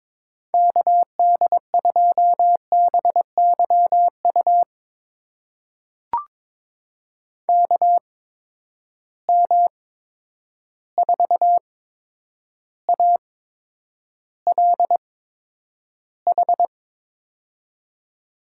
Morse code, heard as a keyed tone, about 22 wpm, spaced as for 9 wpm, 700 Hz. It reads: KD2BYU KM4ALH
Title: USA Call Signs - 6 Characters - 8x Character Spacing - 22wpm